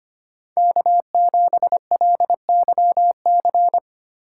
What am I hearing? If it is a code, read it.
Morse code, heard as a keyed tone, about 31 wpm, 700 Hz. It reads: K7LYC
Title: Speed Racing - USA 1x3 Call Signs 25wpm